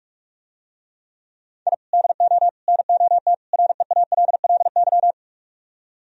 Morse code, heard as a keyed tone, about 47 wpm, 700 Hz. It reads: I DO NOT REALLY